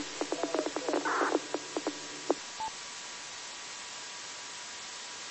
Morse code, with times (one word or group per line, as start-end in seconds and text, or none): none